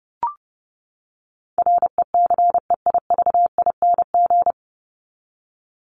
Morse code, heard as a keyed tone, about 30 wpm, 700 Hz. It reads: RECEIVING
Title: ING Suffix - Top 100 Words with Base Words 30wpm